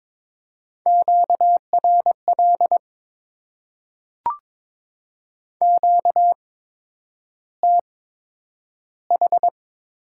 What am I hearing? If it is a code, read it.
Morse code, heard as a keyed tone, about 22 wpm, spaced as for 9 wpm, 700 Hz. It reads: QRL QTH
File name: Top 20 QSO Elements - 8x Character Spacing - 22wpm